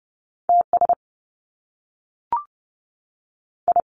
Morse code, heard as a keyed tone, about 30 wpm, 700 Hz. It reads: T S I
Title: Top Two Letter Combinations in Words 30wpm